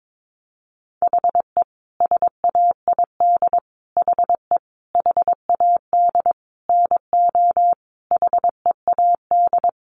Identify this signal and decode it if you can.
Morse code, heard as a keyed tone, about 22 wpm, 700 Hz. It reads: HE SAID HE HAD NO HEAD
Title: Sentences from Top 1500 Words 22wpm